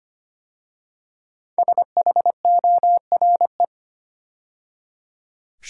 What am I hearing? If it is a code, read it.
Morse code, heard as a keyed tone, about 25 wpm, 700 Hz. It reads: SHORE
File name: 5 Letter Words - First 3 Letters Given - 25wpm